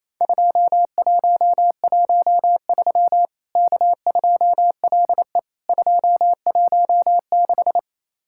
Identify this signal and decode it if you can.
Morse code, heard as a keyed tone, about 28 wpm, 700 Hz. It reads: 2113 K2LE 216
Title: Worked All Europe DX Contest - QTC Exchange - 28wpm